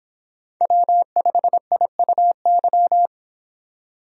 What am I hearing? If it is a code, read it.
Morse code, heard as a keyed tone, about 25 wpm, 700 Hz. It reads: W5IUY